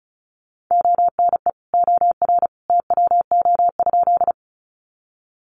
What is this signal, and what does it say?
Morse code, heard as a keyed tone, about 34 wpm, 700 Hz. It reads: ONE OR TWO?
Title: Sentences from Top 100 Words 35wpm